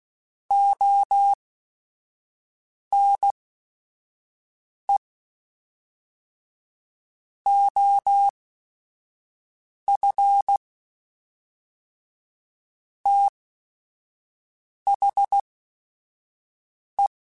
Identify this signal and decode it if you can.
Morse code, heard as a keyed tone, about 16 wpm, spaced as for 5 wpm, 800 Hz. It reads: ONE OF THE
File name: Technical Subjects / Technical Subjects-mp3s / OR Gate @5